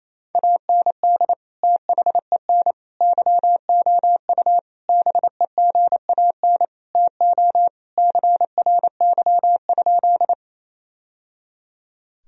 Morse code, heard as a keyed tone, about 27 wpm, 700 Hz. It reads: AND THEN YOU BEGAN TO CRY?
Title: Sentences from Top 1000 Words - Review Part 2 - 28wpm